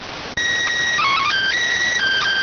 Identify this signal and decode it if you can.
SSTV (slow-scan television), PD 290.